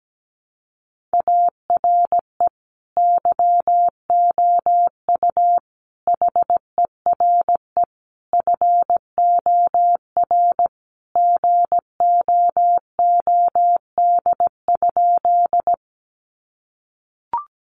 Morse code, heard as a keyed tone, about 17 wpm, 700 Hz. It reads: ARE YOU HERE FOR GOOD?
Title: Sentences from Top 200 Words 17wpm